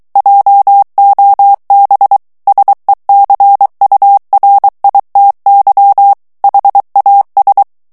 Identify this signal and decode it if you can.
Morse code, heard as a keyed tone, about 23 wpm, 800 Hz. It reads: JOB SECURITY HAS
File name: morse mp3 / ANGLAIS texte clair 1 vitesse 22 mots minute